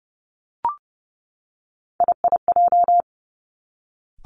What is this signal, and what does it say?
Morse code, heard as a keyed tone, about 30 wpm, 700 Hz. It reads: IIJ
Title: Three Challenging Letter-Number Combinations 30wpm